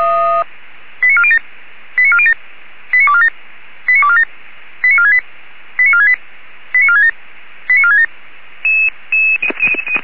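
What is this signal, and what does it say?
Five-tone selective call (ZVEI), heard as quick runs of five tones, tones of about 70 ms. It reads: 89407 89407 89247 89247 89468 89468 89467 89467
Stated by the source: ZVEI five-tone selective call received on SDR.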